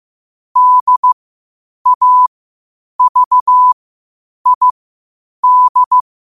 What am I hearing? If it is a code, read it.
Morse code, heard as a keyed tone, about 15 wpm, spaced as for 8 wpm, 1000 Hz. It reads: DAVID
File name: qotdfiles / 8wpm